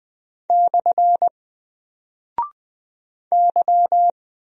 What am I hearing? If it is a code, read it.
Morse code, heard as a keyed tone, about 20 wpm, 700 Hz. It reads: / Y
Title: Single Character (Letter:Number:Punctuation:Prosign) 20wpm